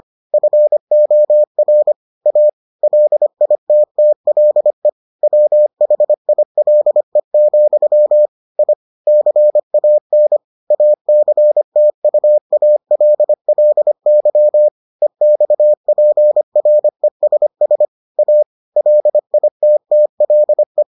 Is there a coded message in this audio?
Yes: Morse code from 0.3 to 20.8 s, heard as a keyed tone, about 25 wpm, 600 Hz. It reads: FOR A LITTLE WHILE, I CAN ACTUALLY EXPRESS A LITTLE